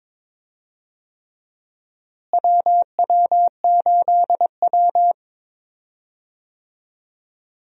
Morse code, heard as a keyed tone, about 22 wpm, 700 Hz. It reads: WW8W
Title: USA Call Signs - 4 Characters - Repeated 3x - 22wpm